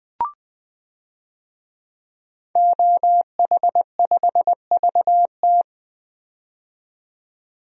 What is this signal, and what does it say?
Morse code, heard as a keyed tone, about 20 wpm, 700 Hz. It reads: OH5VT
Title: DXCC-100 Call Signs - 5 Characters - 20wpm